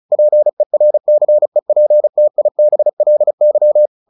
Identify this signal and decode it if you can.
Morse code, heard as a keyed tone, about 34 wpm, 600 Hz. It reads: PERCEPTIBLY
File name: the old man and the sea 2 35wpm